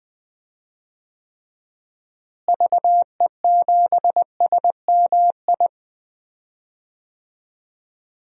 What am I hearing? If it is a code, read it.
Morse code, heard as a keyed tone, about 20 wpm, 700 Hz. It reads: VE7SMI
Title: DXCC-100 Call Signs - 6 Characters - Repeated 2x - 20wpm